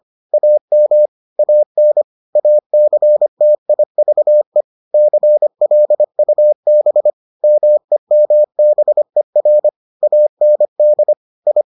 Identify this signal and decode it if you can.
Morse code, heard as a keyed tone, about 25 wpm, 600 Hz. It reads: AM AN ACTIVE CLUB MEMBER AND I